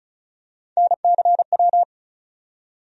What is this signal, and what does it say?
Morse code, heard as a keyed tone, about 36 wpm, 700 Hz. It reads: NCW
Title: USA Call Sign Postfixes - 3 Letters 35wpm